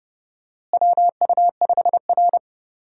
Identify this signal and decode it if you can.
Morse code, heard as a keyed tone, about 32 wpm, 700 Hz. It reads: WU5R
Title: USA Call Signs - 4 Characters - Repeated 3x - 30wpm